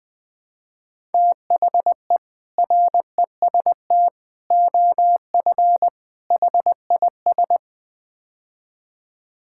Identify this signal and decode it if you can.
Morse code, heard as a keyed tone, about 20 wpm, 700 Hz. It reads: THE REST OF HIS
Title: Build A Phrase - 5-Word Phrases - 20wpm